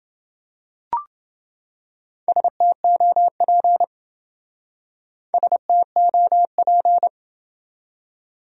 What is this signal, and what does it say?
Morse code, heard as a keyed tone, about 28 wpm, 700 Hz. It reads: STOP STOP